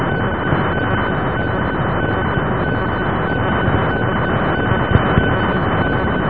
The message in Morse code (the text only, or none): none